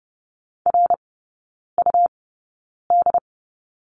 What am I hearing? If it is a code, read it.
Morse code, heard as a keyed tone, about 30 wpm, spaced as for 9 wpm, 700 Hz. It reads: RUD
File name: Lesson 17e - Letter u - Three Characters 30wpm